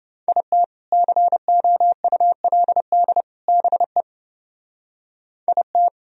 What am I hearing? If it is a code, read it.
Morse code, heard as a keyed tone, about 30 wpm, 700 Hz. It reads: IT COULD BE IT